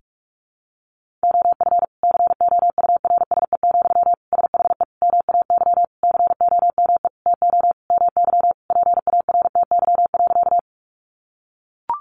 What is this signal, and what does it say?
Morse code, heard as a keyed tone, about 44 wpm, 700 Hz. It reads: OF COURSE, SHE MAY COME TO MY PARTY.
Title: Sentences from Top 1400 Words 45wpm